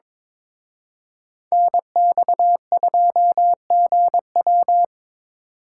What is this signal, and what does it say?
Morse code, heard as a keyed tone, about 25 wpm, 700 Hz. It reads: NX2GW
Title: Speed Racing - USA Call Signs 22wpm